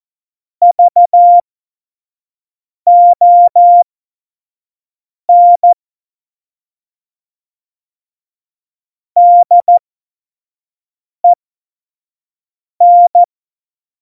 Morse code, heard as a keyed tone, about 14 wpm, spaced as for 5 wpm, 700 Hz. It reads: VON DEN
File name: Morsekurs / Witze / Witz 16. Januar 2018, 25 F70